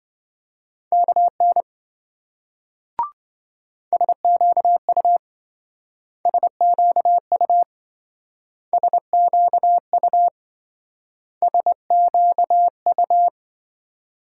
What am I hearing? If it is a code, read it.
Morse code, heard as a keyed tone, about 26 wpm, 700 Hz. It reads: KN SQU SQU SQU SQU